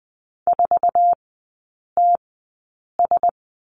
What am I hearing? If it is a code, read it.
Morse code, heard as a keyed tone, about 20 wpm, spaced as for 8 wpm, 700 Hz. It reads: 4TS
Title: 100 Lesson 17e - Letter u - Three Characters 20wpm